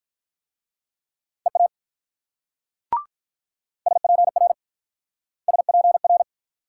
Morse code, heard as a keyed tone, about 47 wpm, 700 Hz. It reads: EA SPR SPR